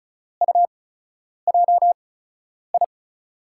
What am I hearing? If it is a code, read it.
Morse code, heard as a keyed tone, about 36 wpm, spaced as for 10 wpm, 700 Hz. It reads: UJI
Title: USA Call Signs - 6 Characters - 8x Character Spacing - 35wpm